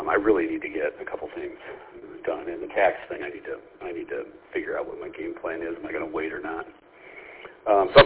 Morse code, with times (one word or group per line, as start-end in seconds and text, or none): none